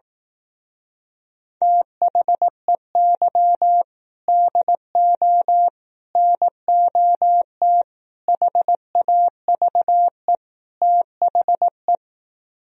Morse code, heard as a keyed tone, about 18 wpm, 700 Hz. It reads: THEY DO NOT HAVE THE
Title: Speed-Racing - Sets of 5 Words 15wpm